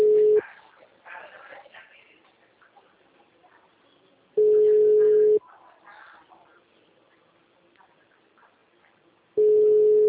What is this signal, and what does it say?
Ringback tone (Europe, 425 Hz).
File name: out- -2005- - - .75894